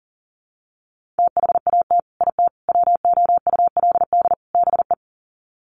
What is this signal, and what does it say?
Morse code, heard as a keyed tone, about 40 wpm, 700 Hz. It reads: THAT IT WOULD BE
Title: Sets of 4 Words 40wpm